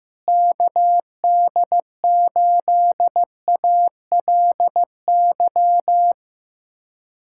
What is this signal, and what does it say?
Morse code, heard as a keyed tone, about 16 wpm, 700 Hz. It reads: KD8ALY
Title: USA Callsign DE Callsign 15wpm